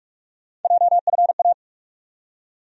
Morse code, heard as a keyed tone, about 46 wpm, 700 Hz. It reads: JFA
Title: USA Call Sign Postfixes - 3 Letters 45wpm